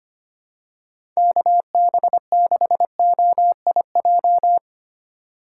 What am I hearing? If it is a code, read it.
Morse code, heard as a keyed tone, about 25 wpm, 700 Hz. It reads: KB6OIJ
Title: USA Call Signs - 6 Characters - 8x Character Spacing - 25wpm